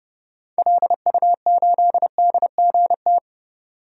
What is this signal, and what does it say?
Morse code, heard as a keyed tone, about 30 wpm, 700 Hz. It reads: LU8DGT
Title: DXCC-100 Call Signs - 6 Characters - 30wpm